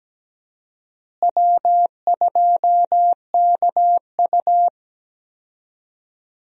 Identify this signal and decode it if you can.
Morse code, heard as a keyed tone, about 17 wpm, 700 Hz. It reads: W2KU
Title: USA Call Signs - 4 Characters - Repeated 3x - 17wpm